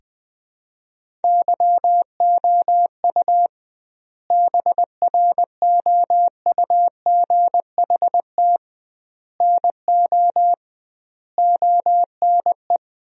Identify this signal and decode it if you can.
Morse code, heard as a keyed tone, about 20 wpm, 700 Hz. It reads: YOU BROUGHT NO ONE
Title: 2x Word-Spacing - Sentences from Top 500 Words - 20wpm